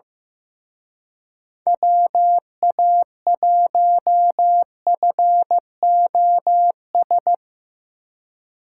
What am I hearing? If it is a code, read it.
Morse code, heard as a keyed tone, about 17 wpm, 700 Hz. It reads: WA1FOS
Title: Speed Racing - USA 2x3 Call Signs 15wpm